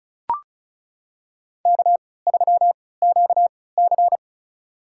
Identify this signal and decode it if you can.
Morse code, heard as a keyed tone, about 36 wpm, spaced as for 20 wpm, 700 Hz. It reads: K3QC
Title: USA Call Signs - 4 Characters - 3x Character Spacing - 35wpm